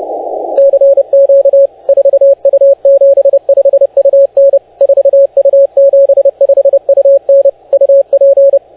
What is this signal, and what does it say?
Morse code, heard as a keyed tone, about 30 wpm, 550 Hz, with radio noise. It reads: CQ 4U75UN 4U75UN UP